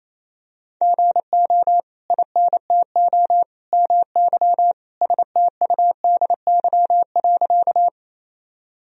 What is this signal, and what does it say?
Morse code, heard as a keyed tone, about 28 wpm, 700 Hz. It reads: GO INTO MY STUDY.
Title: Sentences from Top 1000 Words - Review Part 1 - 28wpm